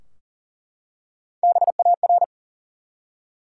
Morse code, heard as a keyed tone, about 41 wpm, 700 Hz. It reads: BAR
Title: Three Letter Words - 40wpm